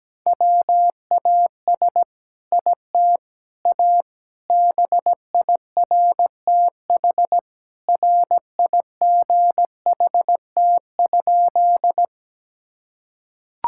Morse code, heard as a keyed tone, about 17 wpm, 700 Hz. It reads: WAS IT A BIRTH RIGHT?
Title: Sentences from Top 1000 Words - Review Part 5 - 17wpm